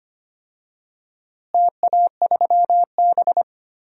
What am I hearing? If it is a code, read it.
Morse code, heard as a keyed tone, about 25 wpm, 700 Hz. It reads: TA3B
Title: DXCC-100 Call Signs - 4 Characters - Repeated 2x - 25wpm